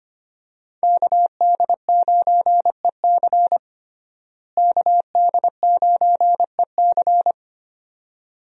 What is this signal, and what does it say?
Morse code, heard as a keyed tone, about 25 wpm, 700 Hz. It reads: KD9EC KD9EC
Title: USA Call Signs - 5 Characters - Repeated 2x - 25wpm